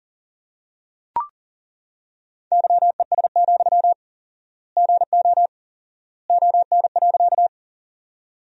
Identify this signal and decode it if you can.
Morse code, heard as a keyed tone, about 38 wpm, 700 Hz. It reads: YES, GO ON.
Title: 4x Word-Spacing - Sentences from Top 500 Words - 40wpm